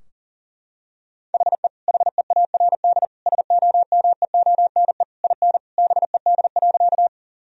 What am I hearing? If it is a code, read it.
Morse code, heard as a keyed tone, about 40 wpm, 700 Hz. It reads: HE HEARD SOMEONE IN BED.
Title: Sentences from Top 1000 Words - Review Part 4 - 40wpm